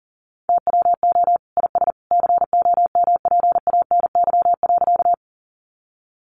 Morse code, heard as a keyed tone, about 40 wpm, 700 Hz. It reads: TWO IS COMPANY.
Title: Sentences from Top 600 Words 40wpm